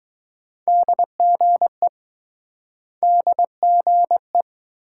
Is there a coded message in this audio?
Yes: Morse code from 0.7 to 4.4 s, heard as a keyed tone, about 20 wpm, 700 Hz. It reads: DGE DGE